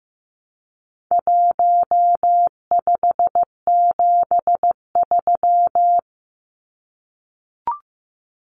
Morse code, heard as a keyed tone, about 15 wpm, 700 Hz. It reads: 1573
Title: Numbers - Contest Style - 4-Digit 15wpm